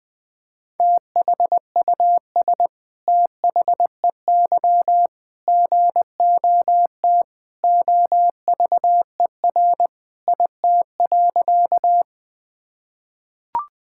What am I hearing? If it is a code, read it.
Morse code, heard as a keyed tone, about 21 wpm, 700 Hz. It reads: THUS THEY GOT OVER IT.
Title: Sentences from Top 1000 Words - Review Part 5 - 20wpm